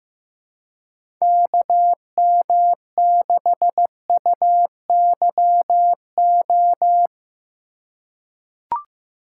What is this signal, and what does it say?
Morse code, heard as a keyed tone, about 15 wpm, 700 Hz. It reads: KM6UYO
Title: USA Call Signs - 6 Characters - 3x Character Spacing - 15wpm